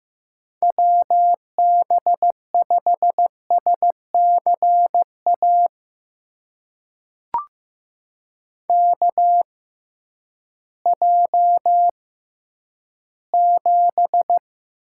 Morse code, heard as a keyed tone, about 15 wpm, spaced as for 10 wpm, 700 Hz. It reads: WB5SCA KJ7
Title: USA Call Signs - 6 Characters - 6x Character Spacing - 15wpm